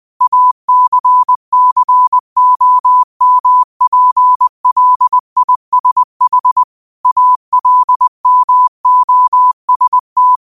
Morse code, heard as a keyed tone, about 20 wpm, 1000 Hz. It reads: ACCOMPLISH ALMOST
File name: qotdfiles / 20wpm